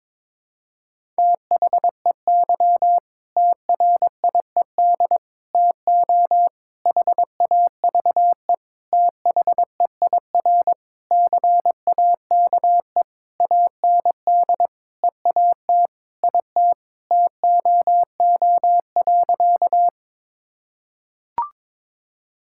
Morse code, heard as a keyed tone, about 22 wpm, 700 Hz. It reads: THEY TRIED TO HAVE THEIR CAKE AND EAT IT TOO.
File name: practice / 22wpm / Sentences with 8-Word Phrases - 22wpm